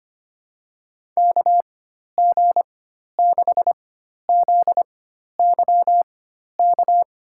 Morse code, heard as a keyed tone, about 25 wpm, spaced as for 12 wpm, 700 Hz. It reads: KG6ZYK
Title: USA Call Signs - 6 Characters - 4x Character Spacing - 25wpm